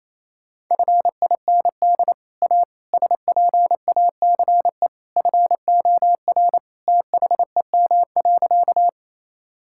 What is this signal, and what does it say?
Morse code, heard as a keyed tone, about 28 wpm, 700 Hz. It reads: FIND A SPACE FOR THEM.
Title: Sentences from Top 1000 Words - Review Part 5 - 28wpm